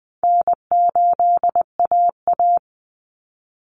Morse code, heard as a keyed tone, about 20 wpm, 700 Hz. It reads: N8AA
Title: Call Signs Repeated 2x - 20wpm